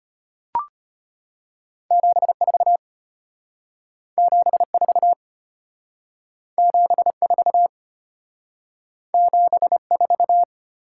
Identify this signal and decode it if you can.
Morse code, heard as a keyed tone, about 32 wpm, 700 Hz. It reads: 74 74 74 74